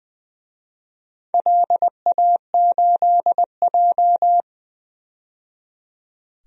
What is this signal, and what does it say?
Morse code, heard as a keyed tone, about 20 wpm, 700 Hz. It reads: LA8J
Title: DXCC-100 Call Signs - 4 Characters - 20wpm